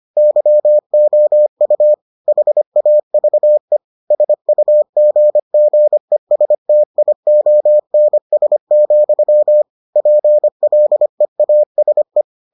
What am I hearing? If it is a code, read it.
Morse code, heard as a keyed tone, about 25 wpm, 600 Hz. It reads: YOU HAVE SUGGESTIONS, PLEASE